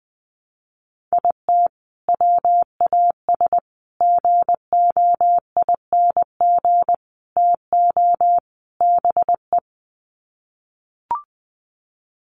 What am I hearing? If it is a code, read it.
Morse code, heard as a keyed tone, about 20 wpm, 700 Hz. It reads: IT WAS GOING TO BE
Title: Sets of 5 Words 20wpm